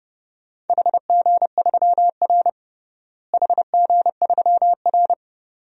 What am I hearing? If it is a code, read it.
Morse code, heard as a keyed tone, about 32 wpm, 700 Hz. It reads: HG3R HG3R
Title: DXCC-100 Call Signs - 4 Characters - Repeated 2x - 30wpm